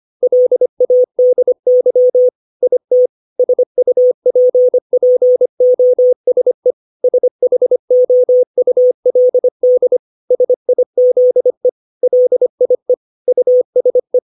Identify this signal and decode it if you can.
Morse code, heard as a keyed tone, about 25 wpm, 500 Hz. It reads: LADY IT SUPPOSE SHOULD SIZE LIE USE
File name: 632 words / 632 Words 25WPM